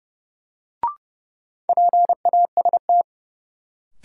Morse code, heard as a keyed tone, about 31 wpm, 700 Hz. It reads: PAST
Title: Sentences from Top 600 Words - New Word Drill - 30wpm